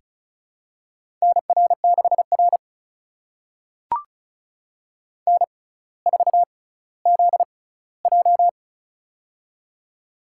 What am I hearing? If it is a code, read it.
Morse code, heard as a keyed tone, about 36 wpm, spaced as for 18 wpm, 700 Hz. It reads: NR6R N4ZJ
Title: USA Call Signs - 4 Characters - 6x Character Spacing - 35wpm